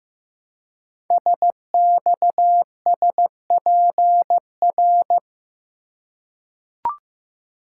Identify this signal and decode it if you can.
Morse code, heard as a keyed tone, about 15 wpm, 700 Hz. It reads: SXSPR
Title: Code Groups - 5 Letters - 15wpm